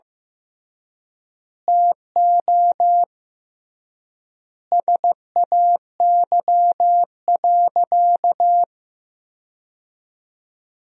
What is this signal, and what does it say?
Morse code, heard as a keyed tone, about 16 wpm, 700 Hz. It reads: TO SAY.
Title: 3x Word-Spacing - Sentences from Top 500 Words - 15wpm